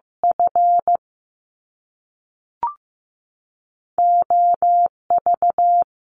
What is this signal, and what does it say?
Morse code, heard as a keyed tone, about 15 wpm, 700 Hz. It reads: F O V
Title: Top Two Letter Combinations in Words 15wpm